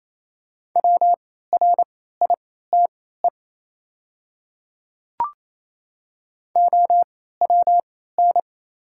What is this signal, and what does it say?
Morse code, heard as a keyed tone, about 27 wpm, spaced as for 16 wpm, 700 Hz. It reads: WRITE OWN
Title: 3x Character Spacing - Top 300 Words - 28wpm